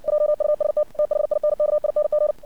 Morse code, heard as a keyed tone, about 56 wpm, 600 Hz. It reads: JUST THINKING
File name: archived / CRSnet